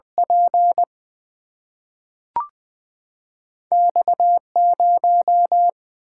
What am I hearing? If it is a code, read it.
Morse code, heard as a keyed tone, about 20 wpm, 700 Hz. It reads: P X 0
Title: Single-Letter Single-Number 20wpm